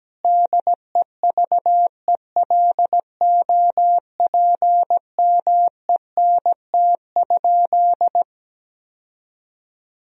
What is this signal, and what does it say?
Morse code, heard as a keyed tone, about 17 wpm, 700 Hz. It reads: DEVELOPMENT?
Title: Sentences from Top 1000 Words - Review Part 4 - 17wpm